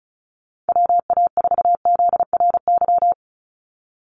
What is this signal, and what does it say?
Morse code, heard as a keyed tone, about 34 wpm, 700 Hz. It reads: WA4ZRY
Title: USA Call Signs 35wpm